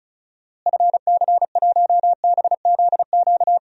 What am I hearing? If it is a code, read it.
Morse code, heard as a keyed tone, about 36 wpm, 700 Hz. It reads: FC1BZQ
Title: DXCC-100 Call Signs - 6 Characters - Repeated 2x - 35wpm